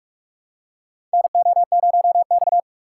Morse code, heard as a keyed tone, about 44 wpm, 700 Hz. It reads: NO0X
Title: USA Call Signs - 6x Character Spacing - 45wpm